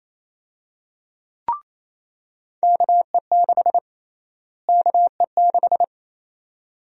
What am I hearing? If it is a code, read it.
Morse code, heard as a keyed tone, about 28 wpm, 700 Hz. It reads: KE6 KE6
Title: USA Call Sign Prefixes - Repeated 2x - 28wpm